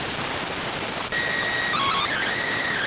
SSTV (slow-scan television), PD 290.